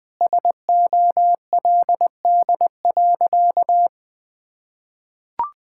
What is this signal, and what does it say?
Morse code, heard as a keyed tone, about 20 wpm, 700 Hz. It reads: SOLD.
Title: Sentences with 4-Word Phrases - 20wpm